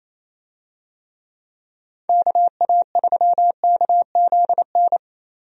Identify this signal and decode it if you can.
Morse code, heard as a keyed tone, about 28 wpm, 700 Hz. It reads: KA3KZN
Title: USA Call Signs - 6 Characters - Repeated 3x - 28wpm